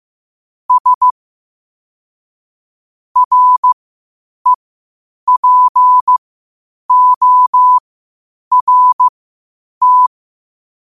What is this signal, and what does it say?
Morse code, heard as a keyed tone, about 15 wpm, spaced as for 8 wpm, 1000 Hz. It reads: S REPORT
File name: qotdfiles / 8wpm